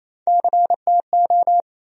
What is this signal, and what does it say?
Morse code, heard as a keyed tone, about 28 wpm, 700 Hz. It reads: CTO
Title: USA Call Sign Postfixes - 3 Letters 28wpm